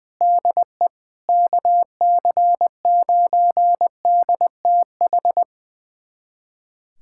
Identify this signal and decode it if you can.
Morse code, heard as a keyed tone, about 20 wpm, 700 Hz. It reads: DE KC9DTH
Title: USA Callsign DE Callsign - 6 Characters - 20wpm